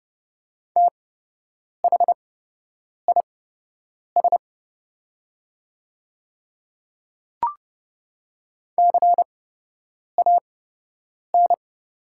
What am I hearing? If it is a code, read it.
Morse code, heard as a keyed tone, about 30 wpm, spaced as for 8 wpm, 700 Hz. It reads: THIS CAN
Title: 8x Character Spacing - Top 100 Words - 30wpm